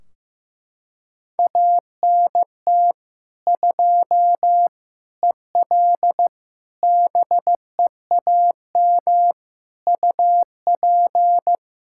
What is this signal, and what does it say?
Morse code, heard as a keyed tone, about 15 wpm, 700 Hz. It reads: ANT 2 EL BEAM UP
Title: Practice QSO - Exchange 2 - Info - 15wpm